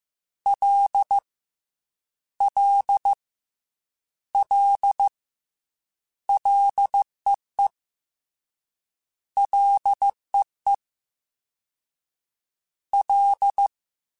Morse code, heard as a keyed tone, about 15 wpm, 800 Hz. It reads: L L L LEE LEE L
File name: Ls / Single L m3s / Single L @13